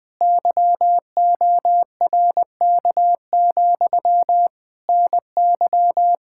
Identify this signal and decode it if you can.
Morse code, heard as a keyed tone, about 20 wpm, 700 Hz. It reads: YORK, NY
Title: Speed-Racing Common QSO Phrases Encore - 17wpm